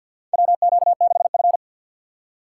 Morse code, heard as a keyed tone, about 49 wpm, 700 Hz. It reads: WQ6F
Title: USA Call Signs - 4 Characters - 2x Character Spacing - 50wpm